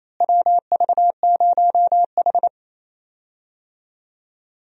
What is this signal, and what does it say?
Morse code, heard as a keyed tone, about 28 wpm, 700 Hz. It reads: WV0H
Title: USA Call Signs - 4 Characters - Repeated 3x - 28wpm